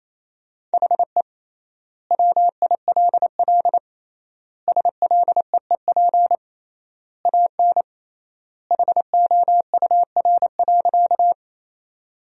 Morse code, heard as a keyed tone, about 28 wpm, 700 Hz. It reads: HE WILL SLEEP AN HOUR.
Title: 3x Word-Spacing - Sentences from Top 500 Words - 28wpm